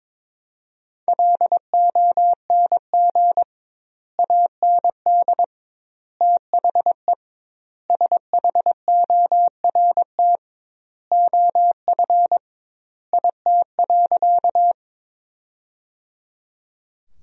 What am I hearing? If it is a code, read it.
Morse code, heard as a keyed tone, about 22 wpm, 700 Hz. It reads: LONG AND THE SHORT OF IT.
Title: 2x Word-Spacing - Sentences from Top 400 Words - 22wpm